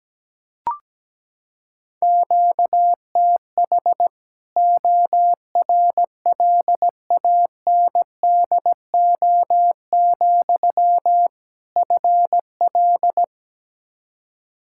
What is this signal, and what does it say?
Morse code, heard as a keyed tone, about 17 wpm, 700 Hz. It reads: QTH ORLANDO, FL
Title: Practice QSO - Exchange 1 - Beginning - 17wpm